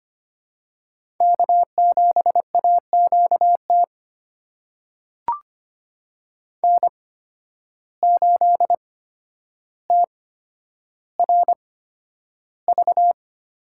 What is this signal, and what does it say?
Morse code, heard as a keyed tone, about 25 wpm, spaced as for 11 wpm, 700 Hz. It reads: K7AQT N8TRV
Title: USA Call Signs - 5 Characters - 8x Character Spacing - 25wpm